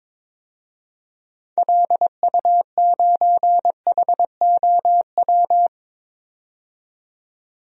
Morse code, heard as a keyed tone, about 22 wpm, 700 Hz. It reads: LU9HOW
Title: DXCC-100 Call Signs - 6 Characters - 22wpm